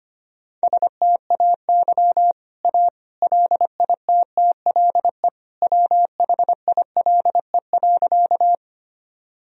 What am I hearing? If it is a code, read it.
Morse code, heard as a keyed tone, about 25 wpm, 700 Hz. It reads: STAY A LITTLE WHILE.
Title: Sentences from Top 1000 Words - Review Part 1 - 25wpm